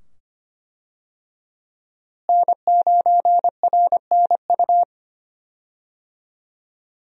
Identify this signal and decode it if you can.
Morse code, heard as a keyed tone, about 25 wpm, 700 Hz. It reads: N9RNU
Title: USA Call Signs - 5 Characters - Repeated 2x - 25wpm